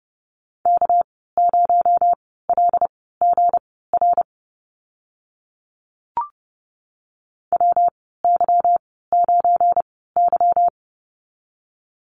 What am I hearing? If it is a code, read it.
Morse code, heard as a keyed tone, about 30 wpm, spaced as for 17 wpm, 700 Hz. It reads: K0LGR WY9Y
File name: Lesson 37g - Letter x - Call Signs 30wpm